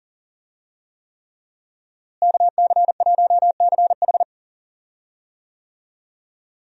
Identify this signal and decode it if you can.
Morse code, heard as a keyed tone, about 40 wpm, 700 Hz. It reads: KC1CH
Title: USA Call Signs - 5 Characters - Repeated 3x - 40wpm